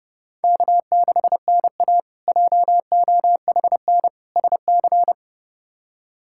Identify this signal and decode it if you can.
Morse code, heard as a keyed tone, about 30 wpm, 700 Hz. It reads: K6NA JOHN SC
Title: K1USN SST QSO Practice - 30wpm